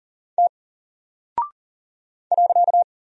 Morse code, heard as a keyed tone, about 40 wpm, 700 Hz. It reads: T .
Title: Single Character (Letter:Number:Punctuation:Prosign) 40wpm